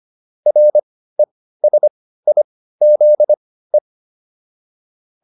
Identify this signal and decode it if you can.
Morse code, heard as a keyed tone, about 25 wpm, spaced as for 15 wpm, 600 Hz. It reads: RESIZE